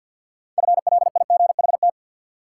Morse code, heard as a keyed tone, about 53 wpm, 700 Hz. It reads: FLIGHT
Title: Sentences from Top 1000 Words - New Word Drill - 50wpm